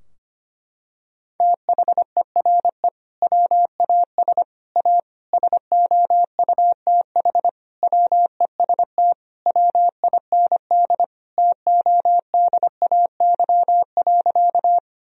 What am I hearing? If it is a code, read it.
Morse code, heard as a keyed tone, about 25 wpm, 700 Hz. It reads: THERE WAS A SOUTH WEST WIND TODAY.
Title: Sentences from Top 1000 Words - Review Part 2 - 25wpm